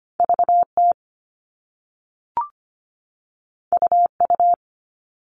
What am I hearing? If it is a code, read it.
Morse code, heard as a keyed tone, about 25 wpm, 700 Hz. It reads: VT UU
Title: All Cut Numbers - 2-Digit 25wpm